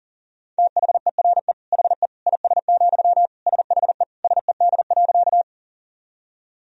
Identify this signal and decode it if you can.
Morse code, heard as a keyed tone, about 37 wpm, 700 Hz. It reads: THERE HE IS, SHE SED.
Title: Sentences from Top 100 Words and Abbreviations 40wpm